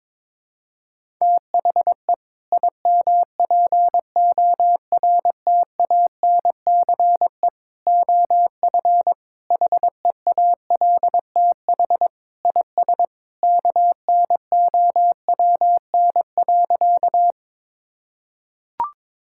Morse code, heard as a keyed tone, about 22 wpm, 700 Hz. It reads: THE IMPORTANCE OF HEALTH IS KNOWN.
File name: practice / 22wpm / Sentences with 3-Word Phrases - 22wpm